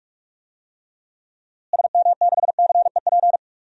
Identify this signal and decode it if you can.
Morse code, heard as a keyed tone, about 44 wpm, 700 Hz. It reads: SM6CEP
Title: DXCC-100 Call Signs - 6 Characters - Repeated 3x - 45wpm